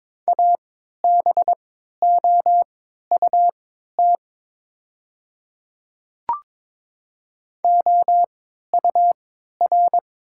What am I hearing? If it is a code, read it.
Morse code, heard as a keyed tone, about 23 wpm, spaced as for 13 wpm, 700 Hz. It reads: ABOUT OUR
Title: 3x Character Spacing - Top 300 Words - 22wpm